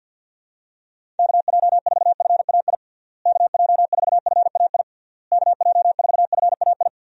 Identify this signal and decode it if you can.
Morse code, heard as a keyed tone, about 50 wpm, 700 Hz. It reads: KJ4FAI KJ4FAI KJ4FAI
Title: USA Call Signs - 6 Characters - Repeated 3x - 50wpm